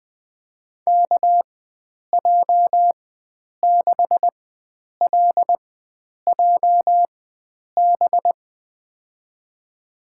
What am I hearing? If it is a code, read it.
Morse code, heard as a keyed tone, about 20 wpm, spaced as for 9 wpm, 700 Hz. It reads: KJ6LJB
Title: USA Call Signs - 6 Characters - 4x Character Spacing - 20wpm